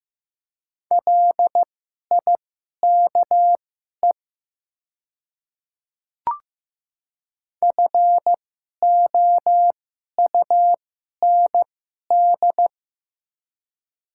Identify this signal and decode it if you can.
Morse code, heard as a keyed tone, about 15 wpm, spaced as for 11 wpm, 700 Hz. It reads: LIKE FOUND
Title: 2x Character Spacing - Top 300 Words - 15wpm